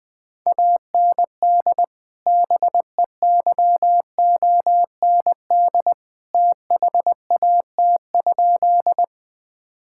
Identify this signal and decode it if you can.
Morse code, heard as a keyed tone, about 20 wpm, 700 Hz. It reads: AND BEYOND THAT?
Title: Sentences from Top 1000 Words - Review Part 6 - 20wpm